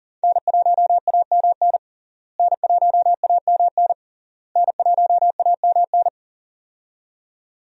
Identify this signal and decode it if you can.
Morse code, heard as a keyed tone, about 40 wpm, 700 Hz. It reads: N1AMN N1AMN N1AMN
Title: USA Call Signs - 5 Characters - Repeated 3x - 40wpm